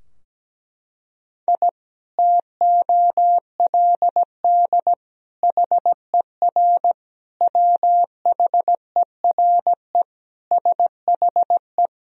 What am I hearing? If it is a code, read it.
Morse code, heard as a keyed tone, about 17 wpm, 700 Hz. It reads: I TOLD HER WHERE SHE
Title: Sentences from Top 1000 Words - Review Part 2 - 17wpm